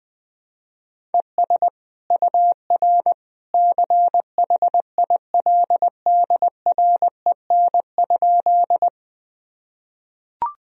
Morse code, heard as a keyed tone, about 20 wpm, 700 Hz. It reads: ES UR CHILDREN?
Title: Sentences from Top 300 Words and Abbreviations 20wpm